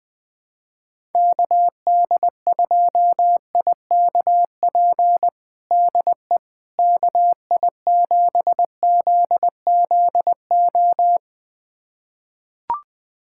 Morse code, heard as a keyed tone, about 20 wpm, 700 Hz. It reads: KD2IKP DE KI7ZZO
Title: USA Callsign DE Callsign - 6 Characters - 20wpm